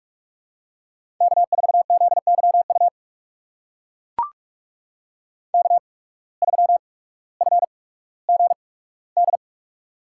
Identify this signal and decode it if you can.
Morse code, heard as a keyed tone, about 44 wpm, spaced as for 19 wpm, 700 Hz. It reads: K4ZYU K3FGD
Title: USA Call Signs - 5 Characters - 8x Character Spacing - 45wpm